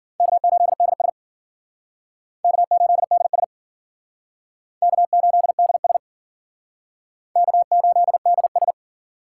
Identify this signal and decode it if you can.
Morse code, heard as a keyed tone, about 50 wpm, 700 Hz. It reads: K8DS K8DS K8DS K8DS
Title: Speed Racing - USA 1x2 Call Signs 40wpm